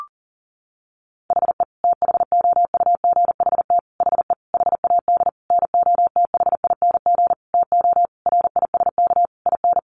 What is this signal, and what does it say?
Morse code, heard as a keyed tone, about 40 wpm, 700 Hz. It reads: HE THOUGHT HE HAD NOTHING TO RISK IN
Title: Sentences from Top 600 Words 40wpm